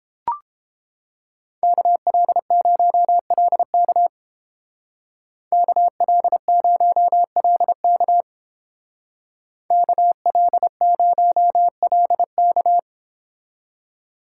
Morse code, heard as a keyed tone, about 30 wpm, 700 Hz. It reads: KL0LK KL0LK KL0LK